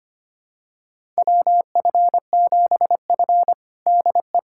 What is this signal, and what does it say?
Morse code, heard as a keyed tone, about 25 wpm, 700 Hz. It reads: WF7F DE
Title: USA Callsign DE Callsign - 4 Characters - 25wpm